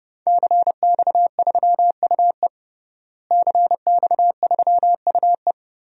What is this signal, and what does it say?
Morse code, heard as a keyed tone, about 30 wpm, 700 Hz. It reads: CX3UE CX3UE
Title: DXCC-100 Call Signs - 5 Characters - Repeated 3x - 30wpm